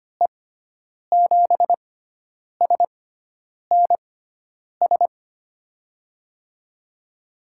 Morse code, heard as a keyed tone, about 25 wpm, spaced as for 9 wpm, 700 Hz. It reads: E7SNS
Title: USA Call Signs - 6 Characters - 6x Character Spacing - 25wpm